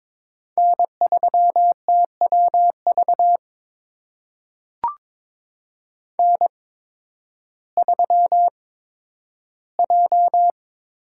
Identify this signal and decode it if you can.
Morse code, heard as a keyed tone, about 22 wpm, spaced as for 12 wpm, 700 Hz. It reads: N3TWV N3J
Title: USA Call Signs - 5 Characters - 8x Character Spacing - 22wpm